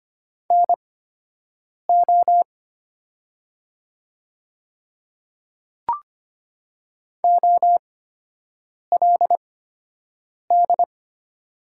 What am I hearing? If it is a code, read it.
Morse code, heard as a keyed tone, about 25 wpm, spaced as for 7 wpm, 700 Hz. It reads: NO OLD
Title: 8x Character Spacing - Top 300 Words - 25wpm